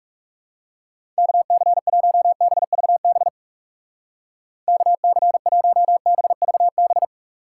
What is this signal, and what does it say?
Morse code, heard as a keyed tone, about 35 wpm, 700 Hz. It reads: KC1BVB KC1BVB
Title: Speed Racing - USA Call Signs 30wpm